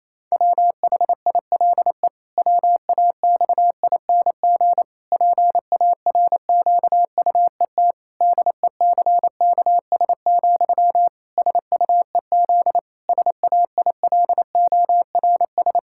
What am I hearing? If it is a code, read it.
Morse code, heard as a keyed tone, about 28 wpm, 700 Hz. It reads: WHILE WAXING PARQUET DECKS, SUEZ SAILORS
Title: Pangram Challenge v2 - 28wpm